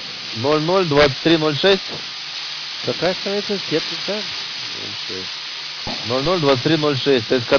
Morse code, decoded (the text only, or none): none